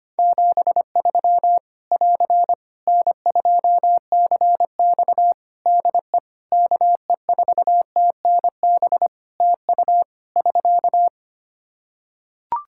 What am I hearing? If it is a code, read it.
Morse code, heard as a keyed tone, about 25 wpm, 700 Hz. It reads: 73 <AR> N2CX DE KE4TNB TU <SK>
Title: Practice QSO - Exchange 4 - Ending - 25wpm